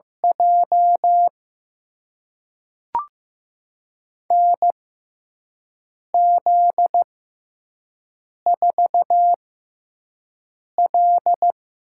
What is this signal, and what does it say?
Morse code, heard as a keyed tone, about 15 wpm, spaced as for 5 wpm, 700 Hz. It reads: J NZ4L
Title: USA Call Signs - 6x Character Spacing - 15wpm